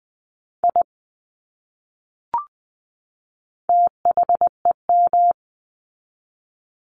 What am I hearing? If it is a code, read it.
Morse code, heard as a keyed tone, about 19 wpm, 700 Hz. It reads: I THEM
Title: Top 200 Words 20wpm